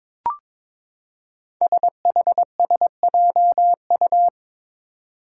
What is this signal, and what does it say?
Morse code, heard as a keyed tone, about 22 wpm, 700 Hz. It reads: SHSJU
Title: Five Challenging Letter-Number Combinations 22wpm